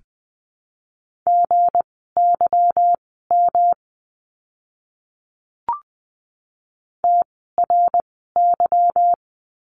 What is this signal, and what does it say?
Morse code, heard as a keyed tone, about 20 wpm, spaced as for 15 wpm, 700 Hz. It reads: GYM TRY
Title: 160 Lesson 26f - Letter g - Words 20wpm